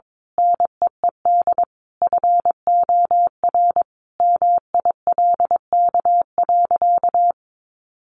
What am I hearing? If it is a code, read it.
Morse code, heard as a keyed tone, about 22 wpm, 700 Hz. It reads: NEED FOR MILK.
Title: Sentences from Top 1500 Words 22wpm